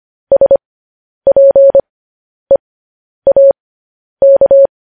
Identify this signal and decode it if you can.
Morse code, heard as a keyed tone, about 25 wpm, spaced as for 10 wpm, 550 Hz. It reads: SPEAK
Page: news-headlines-in-morse-code-at-10-wpm